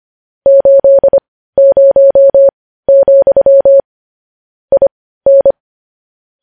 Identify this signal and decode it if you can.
Morse code, heard as a keyed tone, about 25 wpm, spaced as for 15 wpm, 550 Hz. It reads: 80, IN
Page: news-headlines-in-morse-code-at-15-wpm